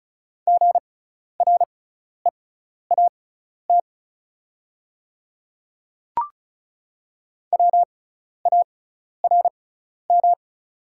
Morse code, heard as a keyed tone, about 34 wpm, spaced as for 12 wpm, 690 Hz. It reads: GREAT WARM
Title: 6x Character Spacing - Top 500 Words - 35wpm